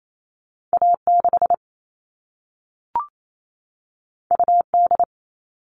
Morse code, heard as a keyed tone, about 28 wpm, 700 Hz. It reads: A6 UD
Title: All Cut Numbers - 2-Digit 28wpm